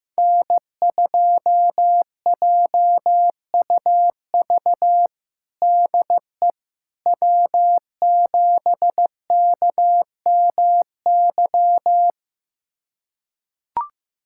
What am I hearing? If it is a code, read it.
Morse code, heard as a keyed tone, about 15 wpm, 700 Hz. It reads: N2JUV DE W7KMY
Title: USA Callsign DE Callsign - 5 Characters - 15wpm